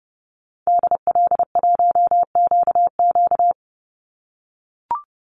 Morse code, heard as a keyed tone, about 29 wpm, 700 Hz. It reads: DL1QQ
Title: Call Signs - 30wpm